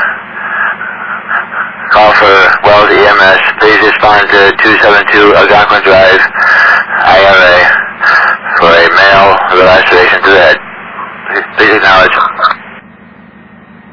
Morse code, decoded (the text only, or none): none